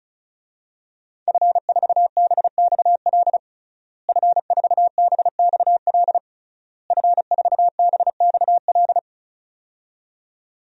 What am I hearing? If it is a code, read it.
Morse code, heard as a keyed tone, about 36 wpm, 700 Hz. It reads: F4BXL F4BXL F4BXL
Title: DXCC-100 Call Signs - 5 Characters - Repeated 3x - 35wpm